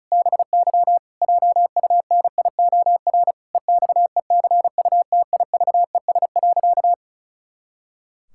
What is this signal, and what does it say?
Morse code, heard as a keyed tone, about 36 wpm, 700 Hz. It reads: BY JUNIOR EXECUTIVES.
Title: Pangram Challenge v2 - 35wpm